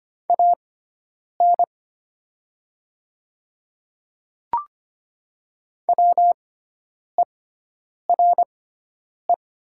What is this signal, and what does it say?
Morse code, heard as a keyed tone, about 25 wpm, spaced as for 9 wpm, 700 Hz. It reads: AN WERE
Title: 6x Character Spacing - Top 100 Words - 25wpm